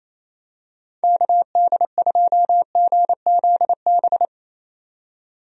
Morse code, heard as a keyed tone, about 28 wpm, 700 Hz. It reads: KD2GZB
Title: USA Call Signs - 6 Characters - 28wpm